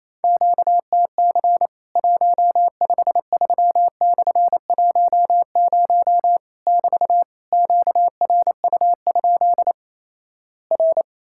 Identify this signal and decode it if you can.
Morse code, heard as a keyed tone, about 28 wpm, 700 Hz. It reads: QTC 153/10 <BT> QRU? R
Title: Worked All Europe DX Contest - QTC Exchange - 28wpm